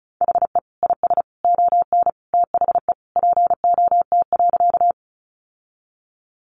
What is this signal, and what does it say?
Morse code, heard as a keyed tone, about 34 wpm, 700 Hz. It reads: HE IS ON THE POT.
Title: Sentences from Top 1000 Words 35wpm